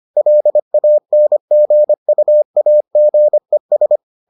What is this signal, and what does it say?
Morse code, heard as a keyed tone, about 25 wpm, 600 Hz. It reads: LANGUAGES